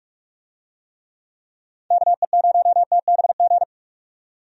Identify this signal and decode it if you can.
Morse code, heard as a keyed tone, about 46 wpm, 700 Hz. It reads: KE0TBG
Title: USA Call Signs - 6 Characters - 45wpm